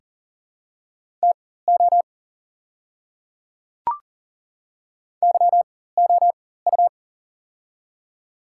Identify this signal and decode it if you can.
Morse code, heard as a keyed tone, about 40 wpm, spaced as for 19 wpm, 690 Hz. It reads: TO YOU
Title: 4x Character Spacing - Top 500 Words - 40wpm